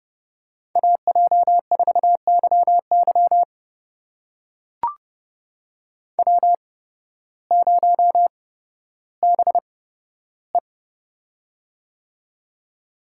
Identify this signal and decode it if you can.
Morse code, heard as a keyed tone, about 30 wpm, spaced as for 14 wpm, 700 Hz. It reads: AJ4YY W0BE
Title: USA Call Signs - 8x Character Spacing - 30wpm